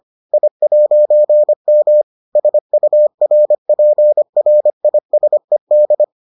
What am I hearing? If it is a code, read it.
Morse code, heard as a keyed tone, about 25 wpm, 600 Hz. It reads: I'M SURPRISED